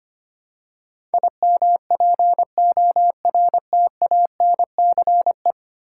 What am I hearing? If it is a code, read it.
Morse code, heard as a keyed tone, about 25 wpm, 700 Hz. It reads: IMPORTANCE
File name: Ten Letter Words - 25wpm